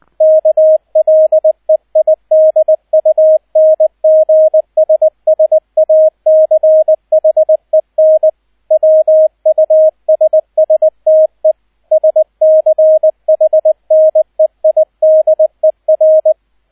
Morse code, heard as a keyed tone, about 20 wpm, 630 Hz, with radio noise. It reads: KLEIDUNGSSACHEN WUSSTE SCHNEIDER